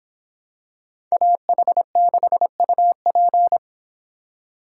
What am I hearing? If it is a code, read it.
Morse code, heard as a keyed tone, about 26 wpm, 700 Hz. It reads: AH6UP